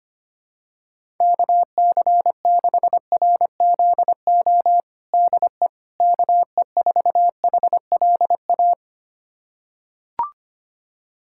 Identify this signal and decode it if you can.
Morse code, heard as a keyed tone, about 25 wpm, 700 Hz. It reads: KC6RZO DE KE4HLA
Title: USA Callsign DE Callsign - 6 Characters - 25wpm